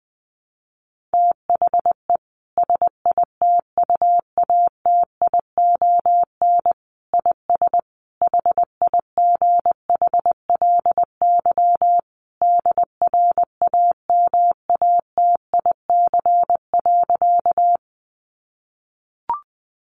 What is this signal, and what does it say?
Morse code, heard as a keyed tone, about 20 wpm, 700 Hz. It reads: THE SITUATION IS HIGHLY DRAMATIC.
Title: Sentences from Top 1300 Words 20wpm